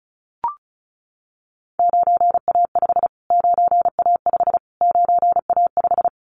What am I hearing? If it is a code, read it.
Morse code, heard as a keyed tone, about 34 wpm, 700 Hz. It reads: 9A5 9A5 9A5
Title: Callsign Prefixes for Top 25 DXCC - Repeated 3x - 35wpm